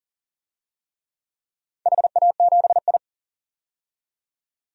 Morse code, heard as a keyed tone, about 40 wpm, 700 Hz. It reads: HA7I
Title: DXCC-100 Call Signs - 4 Characters - 40wpm